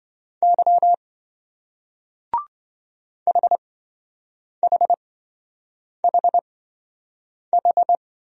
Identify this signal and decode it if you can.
Morse code, heard as a keyed tone, about 26 wpm, spaced as for 7 wpm, 700 Hz. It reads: Y HHHH